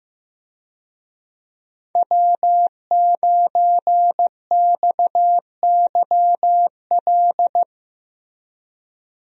Morse code, heard as a keyed tone, about 15 wpm, 700 Hz. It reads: W9XYL
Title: USA Call Signs - 5 Characters - Repeated 2x - 15wpm